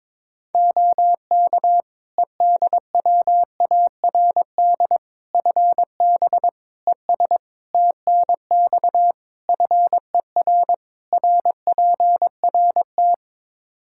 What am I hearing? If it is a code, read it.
Morse code, heard as a keyed tone, about 22 wpm, 700 Hz. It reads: OK EDWARD FB ES TNX FER RPRT
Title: Practice QSO - Exchange 2 - Info - 22wpm